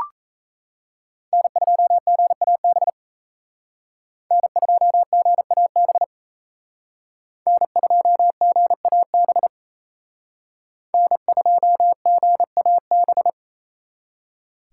Morse code, heard as a keyed tone, about 35 wpm, 700 Hz. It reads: N2GAB N2GAB N2GAB N2GAB
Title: Speed Racing - USA Call Signs 28wpm